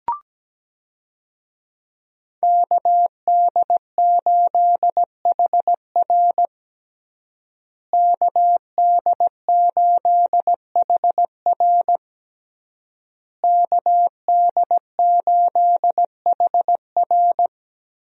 Morse code, heard as a keyed tone, about 17 wpm, 700 Hz. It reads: KD8HR KD8HR KD8HR
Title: USA Call Signs - 5 Characters - Repeated 3x - 17wpm